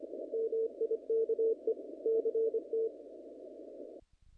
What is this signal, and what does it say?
Morse code, heard as a keyed tone, about 25 wpm, 450 Hz. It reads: MIKE CT
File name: contests / naqp-cw-k5tr-op-k5pi- -2000